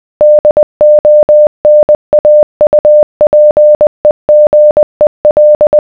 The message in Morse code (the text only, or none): DONAUPEGEL